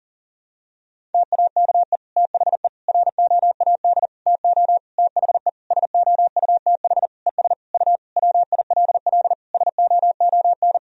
Morse code, heard as a keyed tone, about 40 wpm, 700 Hz. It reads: TAKE THE ROAD TO THE SOUTH ES U WILL SOON
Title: Sentences from Top 400 Words and Abbreviations 40wpm